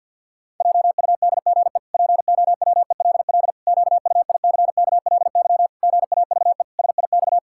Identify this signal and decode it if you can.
Morse code, heard as a keyed tone, about 52 wpm, 700 Hz. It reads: JUDGE POWELL QUICKLY GAVE SIX
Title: Pangram Challenge v2 - 50wpm